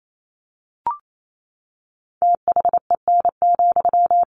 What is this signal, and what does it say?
Morse code, heard as a keyed tone, about 28 wpm, 700 Hz. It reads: THEN,
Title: Sentences from Top 1400 Words 28wpm